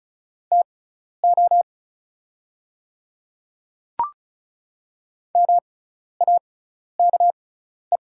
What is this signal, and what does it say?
Morse code, heard as a keyed tone, about 36 wpm, spaced as for 12 wpm, 700 Hz. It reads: TO MAKE
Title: 6x Character Spacing - Top 100 Words - 35wpm